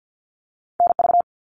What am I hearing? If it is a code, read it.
Morse code, heard as a keyed tone, about 48 wpm, 700 Hz. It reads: NV
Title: US States Repeated 3x 50wpm